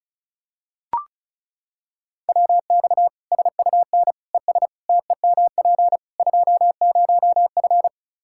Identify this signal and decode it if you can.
Morse code, heard as a keyed tone, about 36 wpm, 700 Hz. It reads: WX SUN ES TEMP 20F
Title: Practice QSO - Exchange 2 - Info - 35wpm